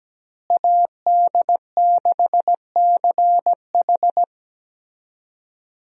Morse code, heard as a keyed tone, about 17 wpm, 700 Hz. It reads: AD6CH
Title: USA Call Signs - 5 Characters - Repeated 2x - 17wpm